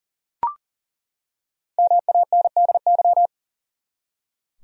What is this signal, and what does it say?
Morse code, heard as a keyed tone, about 40 wpm, 700 Hz. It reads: MANDY
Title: USA Common Names - 40wpm